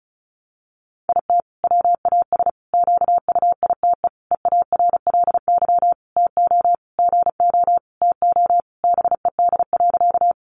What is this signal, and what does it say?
Morse code, heard as a keyed tone, about 34 wpm, 700 Hz. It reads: IT WAS QUITE EARLY TO GO TO BED.
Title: Sentences from Top 700 Words 35wpm